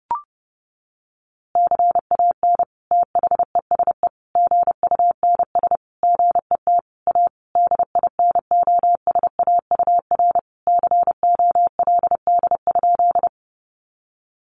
Morse code, heard as a keyed tone, about 30 wpm, 700 Hz. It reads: CAN THESE GUNS GET A DINOSAUR COLD?
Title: A Sound Of Thunder 30wpm